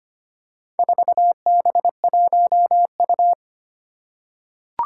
Morse code, heard as a keyed tone, about 25 wpm, 700 Hz. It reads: 4B1U
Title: Four Challenging Letter-Number Combinations 25wpm